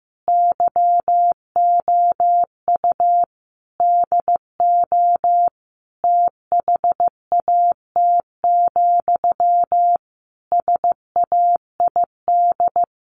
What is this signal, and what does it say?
Morse code, heard as a keyed tone, about 15 wpm, 700 Hz. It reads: YOU DO THAT, SAID
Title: Sentences from Top 1500 Words 15wpm